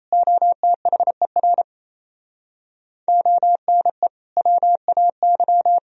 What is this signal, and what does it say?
Morse code, heard as a keyed tone, about 37 wpm, 700 Hz. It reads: OTHER ONE WAY